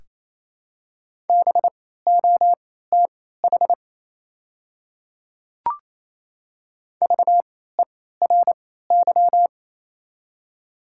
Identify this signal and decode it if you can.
Morse code, heard as a keyed tone, about 28 wpm, spaced as for 16 wpm, 700 Hz. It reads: BOTH VERY
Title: 3x Character Spacing - Top 300 Words - 28wpm